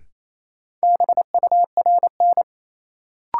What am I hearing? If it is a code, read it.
Morse code, heard as a keyed tone, about 27 wpm, 700 Hz. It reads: BURN
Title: Sentences from Top 1300 Words - New Word Drill - 28wpm